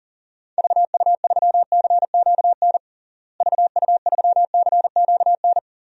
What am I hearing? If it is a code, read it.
Morse code, heard as a keyed tone, about 40 wpm, 700 Hz. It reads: VU3CQN VU3CQN
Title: DXCC-100 Call Signs - 6 Characters - Repeated 3x - 40wpm